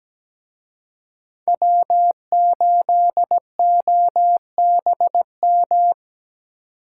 Morse code, heard as a keyed tone, about 21 wpm, 700 Hz. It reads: W8OBM
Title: Speed Racing - USA 1x3 Call Signs 17wpm